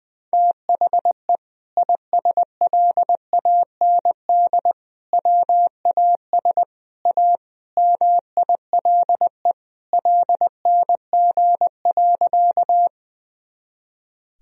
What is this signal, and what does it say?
Morse code, heard as a keyed tone, about 21 wpm, 700 Hz. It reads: THE ISLAND WAS A MILE LNG.
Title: Sentences from Top 500 Words and Abbreviations 20wpm